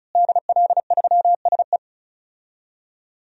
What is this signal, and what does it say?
Morse code, heard as a keyed tone, about 34 wpm, 700 Hz. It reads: DL3SE
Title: DXCC-100 Call Signs - 5 Characters - Repeated 2x - 35wpm